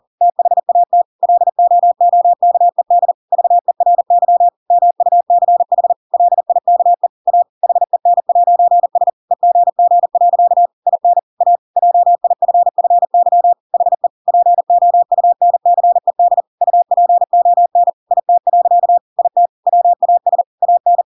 Morse code, heard as a keyed tone, about 40 wpm, 700 Hz. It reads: THAT LOOKED VERY MUCH LIKE A HEN'S EGG. IN A JIFFY HE POUNCED UPON IT. IT WAS AN
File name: Pinocchio / Pinocchio ch5 40 40 ik1hge